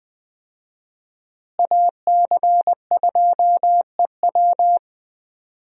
Morse code, heard as a keyed tone, about 21 wpm, 700 Hz. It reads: AC2EW
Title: USA Call Signs - 5 Characters - Repeated 3x - 20wpm